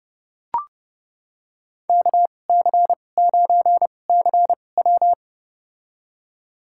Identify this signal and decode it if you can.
Morse code, heard as a keyed tone, about 30 wpm, spaced as for 22 wpm, 700 Hz. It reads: KC9CW
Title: USA Call Signs - 2x Character Spacing - 30wpm